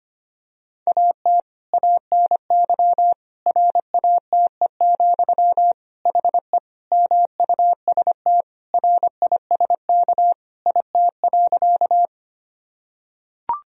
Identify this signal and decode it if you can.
Morse code, heard as a keyed tone, about 25 wpm, 700 Hz. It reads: AT ANY RATE, HE MUST RISK IT.
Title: Sentences from Top 1000 Words - Review Part 4 - 25wpm